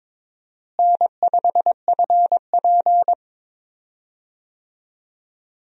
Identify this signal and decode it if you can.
Morse code, heard as a keyed tone, about 22 wpm, 700 Hz. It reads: N5FP
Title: USA Call Signs - 4 Characters - Repeated 2x - 22wpm